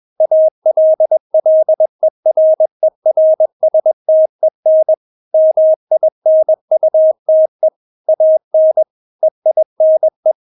Morse code, heard as a keyed tone, about 21 wpm, 630 Hz. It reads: ALLERERSTEN MINUTE AN EINE
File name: Witz2 21wpm